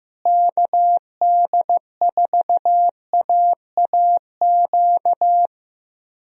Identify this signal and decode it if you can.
Morse code, heard as a keyed tone, about 16 wpm, 700 Hz. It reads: KD4AAQ
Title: USA Call Signs - 6 Characters - Repeated 3x - 15wpm